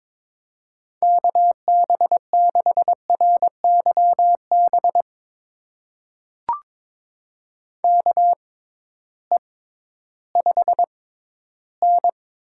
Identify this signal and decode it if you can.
Morse code, heard as a keyed tone, about 22 wpm, spaced as for 13 wpm, 700 Hz. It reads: KB6RYB KE5N
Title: USA Call Signs - 6 Characters - 6x Character Spacing - 22wpm